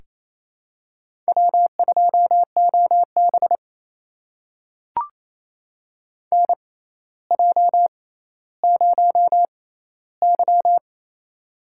Morse code, heard as a keyed tone, about 28 wpm, spaced as for 14 wpm, 700 Hz. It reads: W2OB NJ0Y
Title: USA Call Signs - 6x Character Spacing - 28wpm